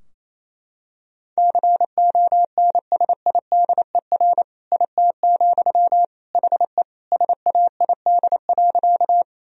Morse code, heard as a keyed tone, about 28 wpm, 700 Hz. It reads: CONSIDER IT, HE SAID.
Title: Sentences from Top 1000 Words - Review Part 4 - 28wpm